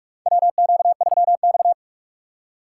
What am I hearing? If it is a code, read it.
Morse code, heard as a keyed tone, about 41 wpm, 690 Hz. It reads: WQ3X
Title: USA Call Signs - 4 Characters - 3x Character Spacing - 45wpm